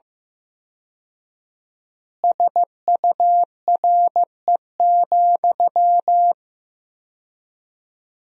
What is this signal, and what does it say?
Morse code, heard as a keyed tone, about 15 wpm, 700 Hz. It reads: SURE,
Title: 4x Word-Spacing - Sentences from Top 500 Words - 15wpm